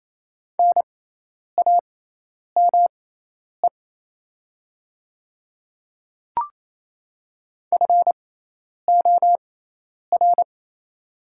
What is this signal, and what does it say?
Morse code, heard as a keyed tone, about 28 wpm, spaced as for 10 wpm, 700 Hz. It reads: NAME FOR
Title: 6x Character Spacing - Top 300 Words - 28wpm